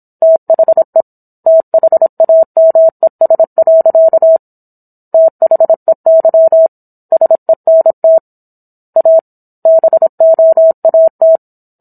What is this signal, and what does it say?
Morse code, heard as a keyed tone, about 27 wpm, 650 Hz. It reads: THE THAMES. THEY SENT A BOAT